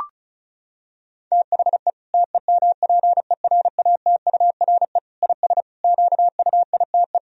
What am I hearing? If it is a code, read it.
Morse code, heard as a keyed tone, about 36 wpm, 700 Hz. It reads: THE TEMPERATURE IS QUITE
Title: Sentences from Top 1000 Words - Review Part 4 - 35wpm